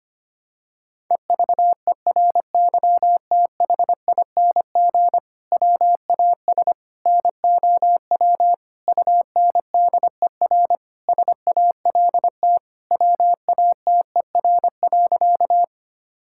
Morse code, heard as a keyed tone, about 25 wpm, 700 Hz. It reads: EVERYTHING WAS NOW UNDER SALT WATER.
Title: Sentences from Top 1000 Words - Review Part 5 - 25wpm